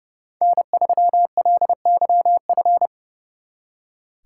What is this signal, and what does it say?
Morse code, heard as a keyed tone, about 31 wpm, 700 Hz. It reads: N3LYF
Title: Speed Racing - USA 1x3 Call Signs 30wpm